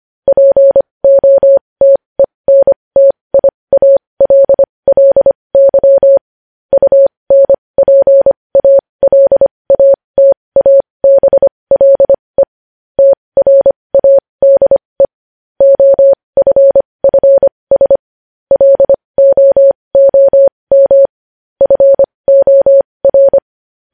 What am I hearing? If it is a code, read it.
Morse code, heard as a keyed tone, about 25 wpm, spaced as for 20 wpm, 550 Hz. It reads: POTENTIALLY UNPALATABLE TRADE OFFS LOOM FOR
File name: podcast / News headlines-1x-25x20